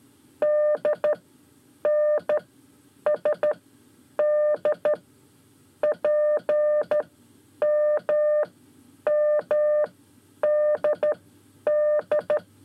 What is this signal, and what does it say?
Morse code, heard as a keyed tone, about 13 wpm, spaced as for 9 wpm, 600 Hz. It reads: DNSD PMMDD
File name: 5 Lezione CW 1